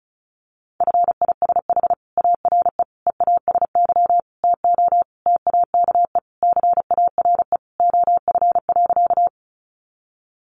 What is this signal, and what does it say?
Morse code, heard as a keyed tone, about 34 wpm, 700 Hz. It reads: FISH ARE EASY TO TAKE CARE OF.
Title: Easy Phrases 35wpm